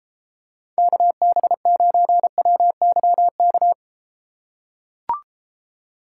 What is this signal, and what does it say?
Morse code, heard as a keyed tone, about 32 wpm, 700 Hz. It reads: KB9WYK